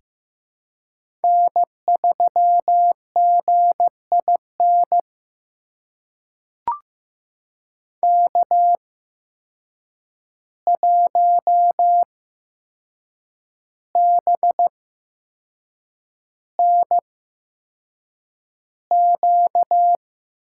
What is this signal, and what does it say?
Morse code, heard as a keyed tone, about 15 wpm, spaced as for 6 wpm, 700 Hz. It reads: N3GIN K1BNQ
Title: USA Call Signs - 5 Characters - 8x Character Spacing - 15wpm